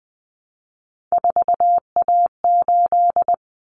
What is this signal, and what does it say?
Morse code, heard as a keyed tone, about 20 wpm, 700 Hz. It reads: 4A8
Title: Common Cut Numbers - 3-Digit 20wpm